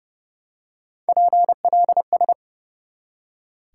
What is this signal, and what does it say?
Morse code, heard as a keyed tone, about 32 wpm, 700 Hz. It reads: PLS
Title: Top 90 QSO Elements 30wpm